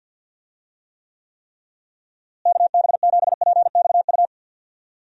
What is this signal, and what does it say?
Morse code, heard as a keyed tone, about 52 wpm, 700 Hz. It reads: KB7PXU
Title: USA Call Signs - 6 Characters - Repeated 3x - 50wpm